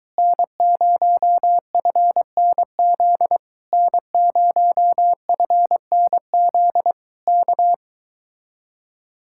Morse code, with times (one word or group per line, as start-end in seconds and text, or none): 0.2-3.4 N0FNZ
3.7-6.9 N0FNZ
7.3-7.7 K